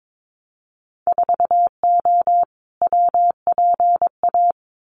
Morse code, heard as a keyed tone, about 22 wpm, 700 Hz. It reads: 4O WPA
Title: Winter Field Day Exchanges 22wpm